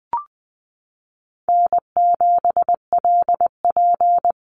Morse code, heard as a keyed tone, about 20 wpm, 700 Hz. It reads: N7LP
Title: NA SOTA Chaser Callsigns 20wpm